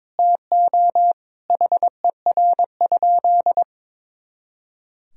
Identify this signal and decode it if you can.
Morse code, heard as a keyed tone, about 22 wpm, 700 Hz. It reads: TO HER?
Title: Sentences from Top 200 Words and Abbreviations 22wpm